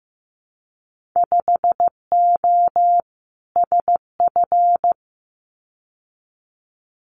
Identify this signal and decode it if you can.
Morse code, heard as a keyed tone, about 15 wpm, 700 Hz. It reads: 5O SF
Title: Winter Field Day Exchanges Encore 15wpm